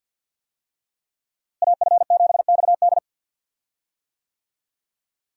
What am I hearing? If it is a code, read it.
Morse code, heard as a keyed tone, about 51 wpm, 700 Hz. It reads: AF7XD
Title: USA Call Signs - 5 Characters - Repeated 2x - 50wpm